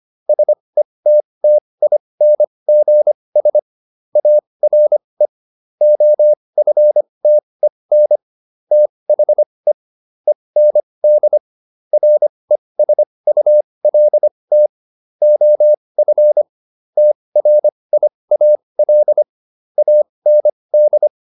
Morse code, heard as a keyed tone, about 25 wpm, spaced as for 20 wpm, 600 Hz. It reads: SETTINGS ARE OFTEN THE END RESULT OF TRIAL AND